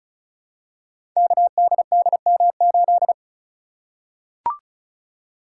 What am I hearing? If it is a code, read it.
Morse code, heard as a keyed tone, about 36 wpm, 700 Hz. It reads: KDDM8
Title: Code Groups - 5 Letter:Numbers - 35wpm